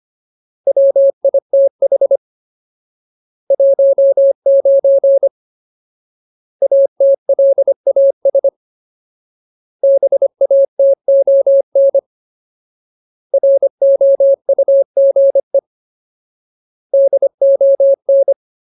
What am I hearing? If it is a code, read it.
Morse code, heard as a keyed tone, about 25 wpm, 550 Hz. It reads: WITH 19 ATLAS BATON ROUGE DON
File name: Ultimate Word List / 25WPM Ultimate